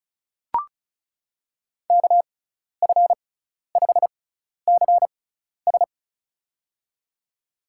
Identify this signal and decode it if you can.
Morse code, heard as a keyed tone, about 36 wpm, spaced as for 12 wpm, 700 Hz. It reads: KF5CS
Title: USA Call Signs - 5 Characters - 6x Character Spacing - 35wpm